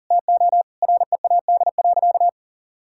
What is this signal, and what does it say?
Morse code, heard as a keyed tone, about 37 wpm, 700 Hz. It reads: TO READ.
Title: Sentences from Top 1000 Words - Review Part 5 - 40wpm